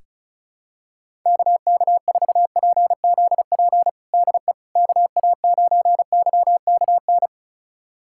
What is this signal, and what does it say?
Morse code, heard as a keyed tone, about 36 wpm, 700 Hz. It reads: KK4PZP DE KA9YKN
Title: USA Callsign DE Callsign - 6 Characters - 35wpm